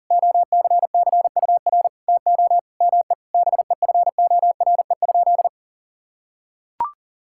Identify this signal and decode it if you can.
Morse code, heard as a keyed tone, about 40 wpm, 700 Hz. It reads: OCCUR TO ME BEFORE?
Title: Sentences from Top 1000 Words - Review Part 6 - 40wpm